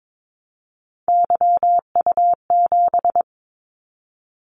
Morse code, heard as a keyed tone, about 22 wpm, 700 Hz. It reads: YU7
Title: Callsign Prefixes for Top 25 DXCC - 22wpm